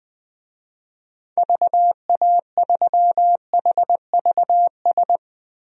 Morse code, heard as a keyed tone, about 20 wpm, 700 Hz. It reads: VA3HVS
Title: DXCC-100 Call Signs - 6 Characters - Repeated 2x - 20wpm